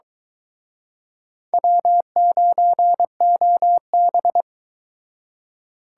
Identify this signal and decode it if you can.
Morse code, heard as a keyed tone, about 23 wpm, 700 Hz. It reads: W9OB